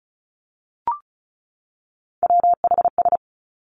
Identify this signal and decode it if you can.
Morse code, heard as a keyed tone, about 36 wpm, 700 Hz. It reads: WHS
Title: Three Challenging Letter-Number Combinations 35wpm